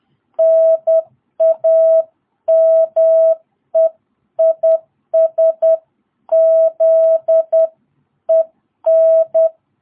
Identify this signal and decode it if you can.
Morse code, heard as a keyed tone, about 10 wpm, 660 Hz. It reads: NAME IS ZEN
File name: jh8cbh / 2022cw